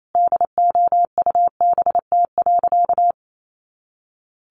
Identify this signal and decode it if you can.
Morse code, heard as a keyed tone, about 28 wpm, 700 Hz. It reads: DOUBT.
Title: Sentences from Top 1500 Words 28wpm